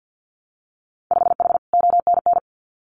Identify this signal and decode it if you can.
Morse code, heard as a keyed tone, about 50 wpm, 700 Hz. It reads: 5H ONN
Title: Winter Field Day Exchanges Encore 50wpm